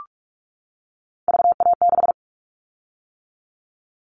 Morse code, heard as a keyed tone, about 46 wpm, 700 Hz. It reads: VA6
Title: Callsign Prefixes for Top 50 DXCC - 45wpm